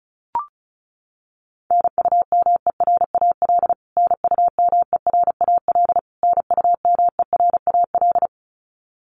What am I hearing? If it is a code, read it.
Morse code, heard as a keyed tone, about 34 wpm, 700 Hz. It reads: NUMERAL NUMERAL NUMERAL
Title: Top 500 Words Repeated 35wpm